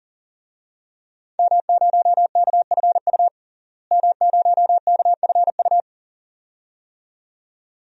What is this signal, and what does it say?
Morse code, heard as a keyed tone, about 39 wpm, 700 Hz. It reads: M0KFU M0KFU
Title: DXCC-100 Call Signs - 5 Characters - Repeated 2x - 40wpm